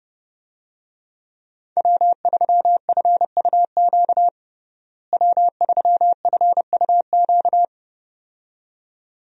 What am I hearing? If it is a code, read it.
Morse code, heard as a keyed tone, about 30 wpm, 700 Hz. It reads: W3FUQ W3FUQ
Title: USA Call Signs - 5 Characters - Repeated 2x - 30wpm